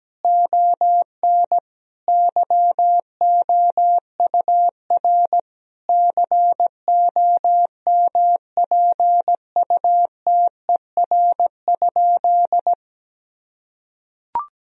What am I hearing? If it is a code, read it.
Morse code, heard as a keyed tone, about 17 wpm, 700 Hz. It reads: ON YOUR COMPUTER?
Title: Sentences from Top 1000 Words - Review Part 3 - 17wpm